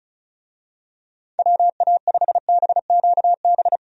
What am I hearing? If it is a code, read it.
Morse code, heard as a keyed tone, about 36 wpm, 700 Hz. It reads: WA5BQB
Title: USA Call Signs - 6 Characters - 35wpm